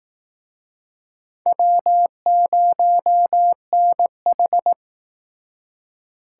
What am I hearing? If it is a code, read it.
Morse code, heard as a keyed tone, about 19 wpm, 700 Hz. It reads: W0NH
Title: Speed Racing - USA 1x2 Call Signs 15wpm